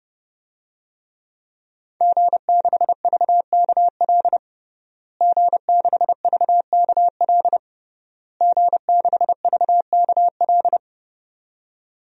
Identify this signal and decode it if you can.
Morse code, heard as a keyed tone, about 32 wpm, 700 Hz. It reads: G6VKL G6VKL G6VKL
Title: DXCC-100 Call Signs - 5 Characters - Repeated 3x - 30wpm